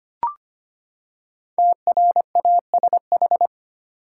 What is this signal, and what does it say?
Morse code, heard as a keyed tone, about 25 wpm, 700 Hz. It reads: TRASH
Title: Five Letter Words - 25wpm